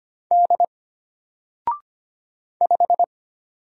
Morse code, heard as a keyed tone, about 25 wpm, 700 Hz. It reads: D 5
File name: Single-Letter-Number-25wpm